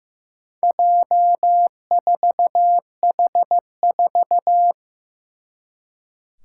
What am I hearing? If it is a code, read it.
Morse code, heard as a keyed tone, about 15 wpm, 700 Hz. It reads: J4H4
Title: Four Challenging Letter-Number Combinations 15wpm